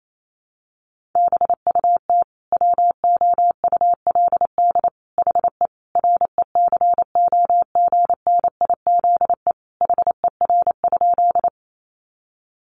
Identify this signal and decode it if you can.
Morse code, heard as a keyed tone, about 28 wpm, 700 Hz. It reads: BUT WOULD HE RECOGNIZE HER?
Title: Sentences from Top 1400 Words 28wpm